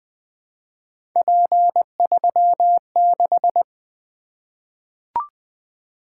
Morse code, heard as a keyed tone, about 20 wpm, 700 Hz. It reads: P36
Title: Three Letter-Number Combinations - 20wpm